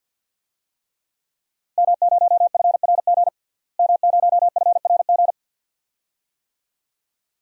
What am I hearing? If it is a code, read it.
Morse code, heard as a keyed tone, about 47 wpm, 700 Hz. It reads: M0FRG M0FRG
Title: DXCC-100 Call Signs - 5 Characters - Repeated 2x - 50wpm